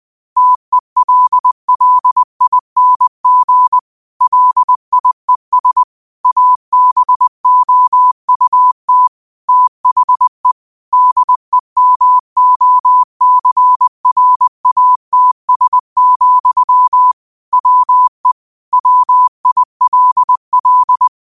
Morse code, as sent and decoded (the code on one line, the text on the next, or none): - . .-.. .-.. .. -. --. / .-.. .. ... / .- -... --- ..- - / - .... / -.. -- --- -.-. .-. .- - ... --..-- / .-- . / .-- .. .-.. .-..
TELLING LIES ABOUT THE DEMOCRATS, WE WILL